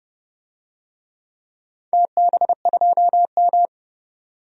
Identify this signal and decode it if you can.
Morse code, heard as a keyed tone, about 32 wpm, 700 Hz. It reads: TB2M
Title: DXCC-100 Call Signs - 4 Characters - 30wpm